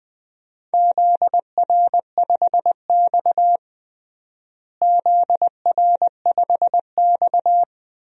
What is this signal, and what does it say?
Morse code, heard as a keyed tone, about 21 wpm, 700 Hz. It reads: ZR5X ZR5X
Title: DXCC-100 Call Signs - 4 Characters - Repeated 3x - 20wpm